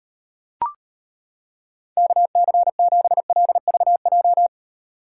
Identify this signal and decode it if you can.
Morse code, heard as a keyed tone, about 32 wpm, 700 Hz. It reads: KC7LVJ